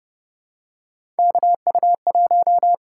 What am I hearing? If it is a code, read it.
Morse code, heard as a keyed tone, about 30 wpm, 700 Hz. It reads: KU1
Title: USA Call Signs - 4 Characters - 8x Character Spacing - 30wpm